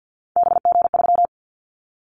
Morse code, heard as a keyed tone, about 50 wpm, 690 Hz. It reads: BG3
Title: Callsign Prefixes for Top 100 DXCC - 50wpm